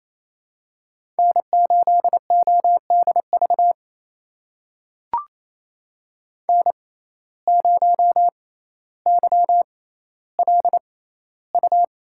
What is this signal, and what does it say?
Morse code, heard as a keyed tone, about 28 wpm, spaced as for 14 wpm, 700 Hz. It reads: N8ODV N0YLU
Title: USA Call Signs - 5 Characters - 6x Character Spacing - 28wpm